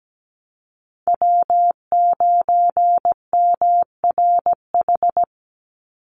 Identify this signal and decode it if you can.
Morse code, heard as a keyed tone, about 17 wpm, 700 Hz. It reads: W9MRH
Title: NA SOTA Chaser Callsigns 17wpm